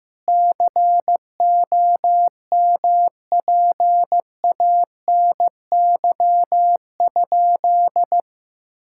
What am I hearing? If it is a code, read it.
Morse code, heard as a keyed tone, about 15 wpm, 700 Hz. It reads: COMPANY?
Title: Sentences from Top 1600 Words 15wpm